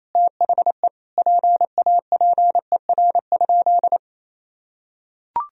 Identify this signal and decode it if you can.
Morse code, heard as a keyed tone, about 28 wpm, 700 Hz. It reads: THE PAPER?
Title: Sentences from Top 1000 Words - Review Part 2 - 28wpm